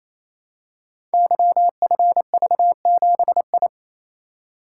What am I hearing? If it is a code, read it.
Morse code, heard as a keyed tone, about 27 wpm, 700 Hz. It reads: YFV7I
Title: Code Groups - 5 Letter:Numbers - 28wpm